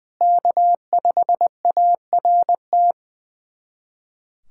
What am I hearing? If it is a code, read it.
Morse code, heard as a keyed tone, about 21 wpm, 700 Hz. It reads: K5ART
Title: USA Callsign DE Callsign - 5 Characters - 20wpm